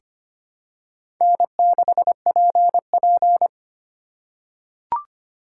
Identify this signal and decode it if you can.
Morse code, heard as a keyed tone, about 25 wpm, 700 Hz. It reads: N6PP
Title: USA Call Signs - 4 Characters - 3x Character Spacing - 25wpm